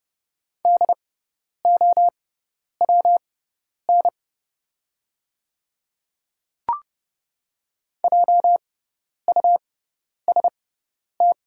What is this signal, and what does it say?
Morse code, heard as a keyed tone, about 32 wpm, spaced as for 10 wpm, 700 Hz. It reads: DOWN JUST
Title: 6x Character Spacing - Top 200 Words - 30wpm